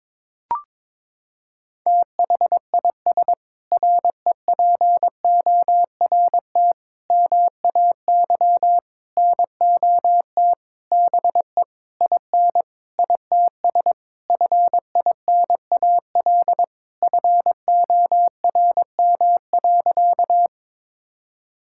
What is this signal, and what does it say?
Morse code, heard as a keyed tone, about 22 wpm, 700 Hz. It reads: THIS REPORT MAY NOT BE IN ITS FINAL FORM.
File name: Sentences with 7-Word Phrases - 22wpm